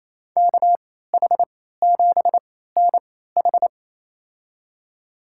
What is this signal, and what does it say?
Morse code, heard as a keyed tone, about 28 wpm, spaced as for 16 wpm, 700 Hz. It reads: KH7NH
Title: USA Call Signs - 3x Character Spacing - 28wpm